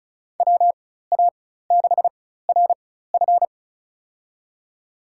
Morse code, heard as a keyed tone, about 34 wpm, spaced as for 16 wpm, 700 Hz. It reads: WA6RF
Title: USA Call Signs - 5 Characters - 4x Character Spacing - 35wpm